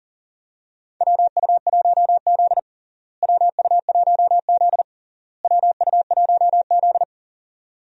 Morse code, heard as a keyed tone, about 40 wpm, 700 Hz. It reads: WU1Z WU1Z WU1Z
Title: USA Call Signs - 4 Characters - Repeated 3x - 40wpm